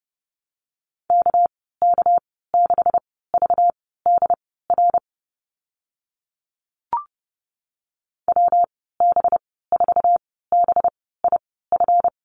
Morse code, heard as a keyed tone, about 30 wpm, spaced as for 17 wpm, 700 Hz. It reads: KK6VDR WB4BIF
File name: Lesson 37g - Letter x - Call Signs 30wpm